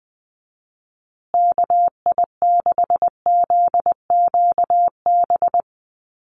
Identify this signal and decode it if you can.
Morse code, heard as a keyed tone, about 20 wpm, 700 Hz. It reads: KI6ZQB
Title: USA Call Signs Repeated 2x 20wpm